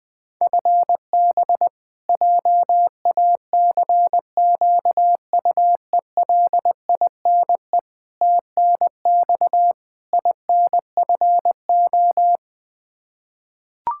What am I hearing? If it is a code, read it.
Morse code, heard as a keyed tone, about 20 wpm, 700 Hz. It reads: FB JACQUELINE TNX INFO
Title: Practice QSO - Exchange 3 - More Info - 20wpm